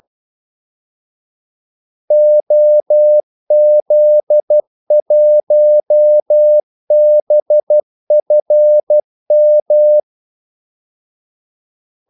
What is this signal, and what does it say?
Morse code, heard as a keyed tone, about 12 wpm, 600 Hz. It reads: OZ1BFM
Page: 2 ekstra ordmellemrum mellem kaldesignalerne